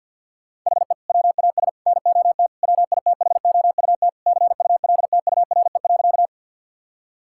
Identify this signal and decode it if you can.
Morse code, heard as a keyed tone, about 53 wpm, 700 Hz. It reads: HE WAS NOT WITHOUT CULTURE.
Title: Sentences from Top 1000 Words - Review Part 6 - 50wpm